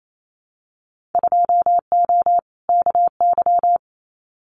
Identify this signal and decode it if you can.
Morse code, heard as a keyed tone, about 28 wpm, 700 Hz. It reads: 2O KY
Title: Winter Field Day Exchanges 28wpm